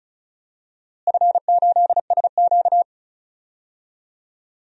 Morse code, heard as a keyed tone, about 34 wpm, 700 Hz. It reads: F8SQ
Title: DXCC-100 Call Signs - 4 Characters - Repeated 2x - 35wpm